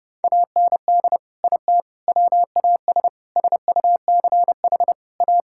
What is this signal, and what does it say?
Morse code, heard as a keyed tone, about 30 wpm, 700 Hz. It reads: AND IT WAS SUCH A
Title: Sentences from Top 1600 Words 30wpm